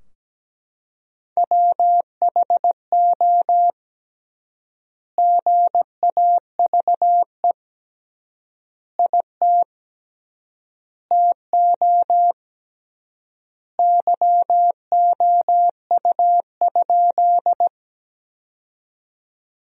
Morse code, heard as a keyed tone, about 17 wpm, 700 Hz. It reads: WHO GAVE IT TO YOU?
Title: 3x Word-Spacing - Sentences from Top 500 Words - 17wpm